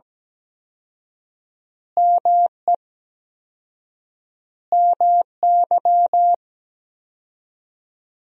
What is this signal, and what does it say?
Morse code, heard as a keyed tone, about 17 wpm, 700 Hz. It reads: ME MY
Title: 4x Word-Spacing - Sentences from Top 500 Words - 17wpm